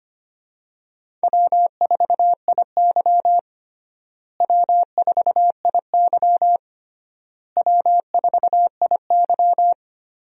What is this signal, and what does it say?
Morse code, heard as a keyed tone, about 25 wpm, 700 Hz. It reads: W4IY W4IY W4IY
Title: USA Call Signs - 4 Characters - Repeated 3x - 25wpm